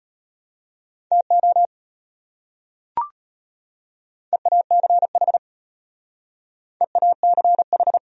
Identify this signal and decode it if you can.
Morse code, heard as a keyed tone, about 33 wpm, 700 Hz. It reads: TO EACH EACH